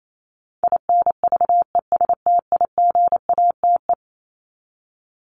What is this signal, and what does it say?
Morse code, heard as a keyed tone, about 28 wpm, 700 Hz. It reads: INVESTIGATE
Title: Top 1751-2000 Words 28wpm